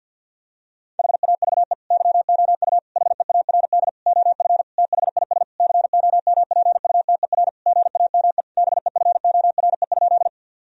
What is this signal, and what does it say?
Morse code, heard as a keyed tone, about 50 wpm, 700 Hz. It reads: HAVE YOU HEARD OF THIS COMPUTER GAME BEFORE?
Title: Sentences from Top 1000 Words - Review Part 3 - 50wpm